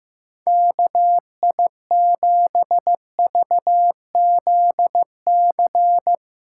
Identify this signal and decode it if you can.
Morse code, heard as a keyed tone, about 16 wpm, 700 Hz. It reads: KI7VZC
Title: USA Call Signs - 6 Characters - 4x Character Spacing - 15wpm